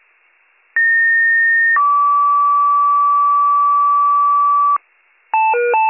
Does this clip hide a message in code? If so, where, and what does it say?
Yes: two-tone page from 0.8 to 4.8 s, heard as a one-second tone, then a three-second tone, about 1850 then 1150 Hz, with noise.